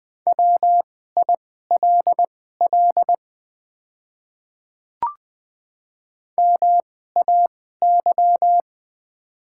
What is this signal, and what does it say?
Morse code, heard as a keyed tone, about 20 wpm, spaced as for 14 wpm, 700 Hz. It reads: WILL MAY
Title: 2x Character Spacing - Top 200 Words - 20wpm